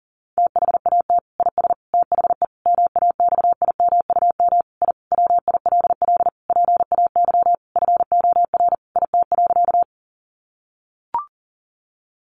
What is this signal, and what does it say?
Morse code, heard as a keyed tone, about 40 wpm, 700 Hz. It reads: THAT IS THE MAXIMUM I WILL PAY FOR IT.
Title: Sentences from Top 1300 Words 40wpm